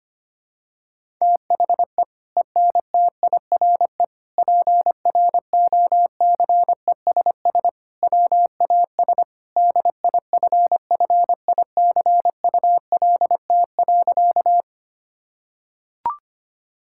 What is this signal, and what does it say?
Morse code, heard as a keyed tone, about 25 wpm, 700 Hz. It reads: THE ENTIRE PROCESS WAS DIFFICULT.
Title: Sentences from Top 1000 Words - Review Part 5 - 25wpm